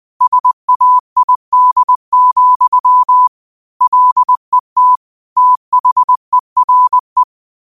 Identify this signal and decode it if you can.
Morse code, heard as a keyed tone, about 20 wpm, 1000 Hz. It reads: SAID, LET THERE